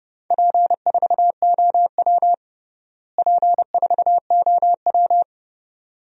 Morse code, heard as a keyed tone, about 30 wpm, 700 Hz. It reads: P4OW P4OW
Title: DXCC-100 Call Signs - 4 Characters - Repeated 2x - 30wpm